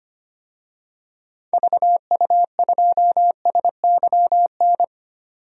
Morse code, heard as a keyed tone, about 25 wpm, 700 Hz. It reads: VU2SYN
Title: DXCC-100 Call Signs - 6 Characters - 25wpm